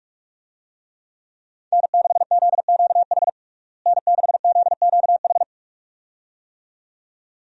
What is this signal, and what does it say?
Morse code, heard as a keyed tone, about 46 wpm, 700 Hz. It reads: N6ZQH N6ZQH
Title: USA Call Signs - 5 Characters - Repeated 2x - 45wpm